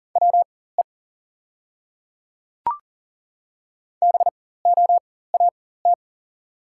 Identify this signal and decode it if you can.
Morse code, heard as a keyed tone, about 40 wpm, spaced as for 19 wpm, 690 Hz. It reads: WE BOAT
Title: 4x Character Spacing - Top 500 Words - 40wpm